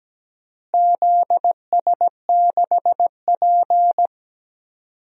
Morse code, heard as a keyed tone, about 17 wpm, 700 Hz. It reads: ZS6P
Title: DXCC-100 Call Signs - 4 Characters - Repeated 3x - 17wpm